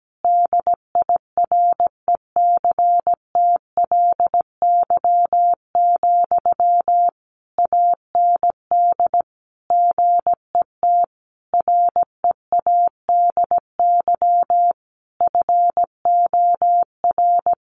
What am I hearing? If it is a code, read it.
Morse code, heard as a keyed tone, about 17 wpm, 700 Hz. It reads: DIRECTLY, AND GET READY FOR
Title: Sentences from Top 800 Words 17wpm